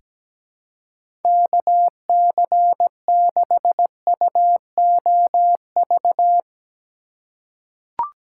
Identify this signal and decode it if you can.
Morse code, heard as a keyed tone, about 17 wpm, 700 Hz. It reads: KC6UOV
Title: USA Call Signs - 6 Characters - 3x Character Spacing - 17wpm